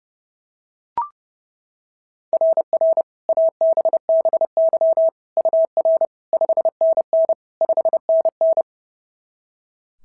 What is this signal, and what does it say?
Morse code, heard as a keyed tone, about 30 wpm, 660 Hz. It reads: RR ABBY UR 5NN 5NN
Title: SOTA QSO Practice 30wpm